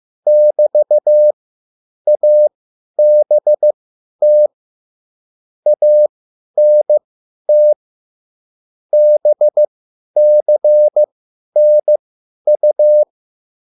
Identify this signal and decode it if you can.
Morse code, heard as a keyed tone, about 15 wpm, spaced as for 10 wpm, 600 Hz. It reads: <BT> ABT ANT BCNU
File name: QSL / CW lesson